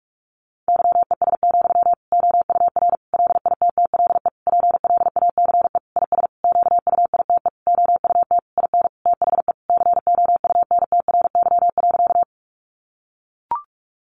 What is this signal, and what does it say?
Morse code, heard as a keyed tone, about 44 wpm, 700 Hz. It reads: YES, OUR LITTLE PLACE IS QUITE OUT IN THE COUNTRY.
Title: Sentences from Top 700 Words 45wpm